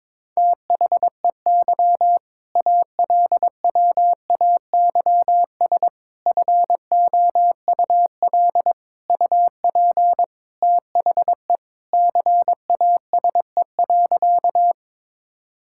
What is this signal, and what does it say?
Morse code, heard as a keyed tone, about 21 wpm, 700 Hz. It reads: THEY ALWAYS FOUL UP THE CASE.
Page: morse-code-ninja-practice-22wpm